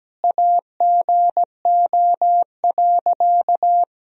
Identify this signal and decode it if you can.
Morse code, heard as a keyed tone, about 17 wpm, 700 Hz. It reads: AGO.
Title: Sentences from Top 1000 Words - Review Part 3 - 17wpm